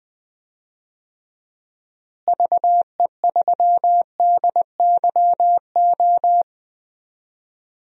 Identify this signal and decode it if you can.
Morse code, heard as a keyed tone, about 20 wpm, 700 Hz. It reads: VE3DYO
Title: DXCC-100 Call Signs - 6 Characters - Repeated 3x - 20wpm